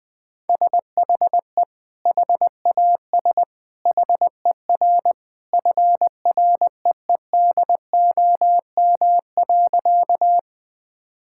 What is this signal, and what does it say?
Morse code, heard as a keyed tone, about 20 wpm, 700 Hz. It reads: SHE HAS HER FREEDOM.
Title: Sentences from Top 1000 Words - Review Part 5 - 20wpm